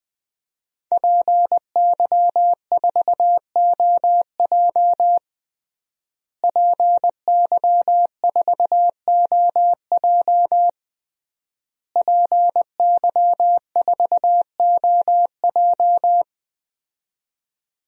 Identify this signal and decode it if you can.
Morse code, heard as a keyed tone, about 20 wpm, 700 Hz. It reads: PY4OJ PY4OJ PY4OJ
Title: DXCC-100 Call Signs - 5 Characters - Repeated 3x - 20wpm